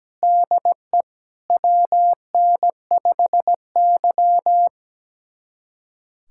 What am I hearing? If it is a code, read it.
Morse code, heard as a keyed tone, about 17 wpm, 700 Hz. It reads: DE WN5Y
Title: USA Callsign DE Callsign - 4 Characters - 17wpm